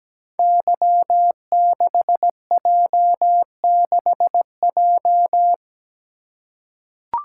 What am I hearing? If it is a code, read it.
Morse code, heard as a keyed tone, about 17 wpm, 700 Hz. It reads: Y6J6J
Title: Code Groups - 5 Letter:Numbers - 17wpm